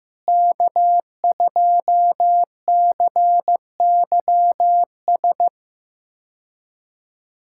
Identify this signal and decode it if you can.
Morse code, heard as a keyed tone, about 15 wpm, 700 Hz. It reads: K2CYS
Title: K1USN SST QSO Practice - 15wpm